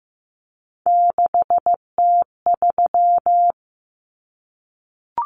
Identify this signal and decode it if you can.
Morse code, heard as a keyed tone, about 15 wpm, 700 Hz. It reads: 6T3
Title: Common Cut Numbers - 3-Digit 15wpm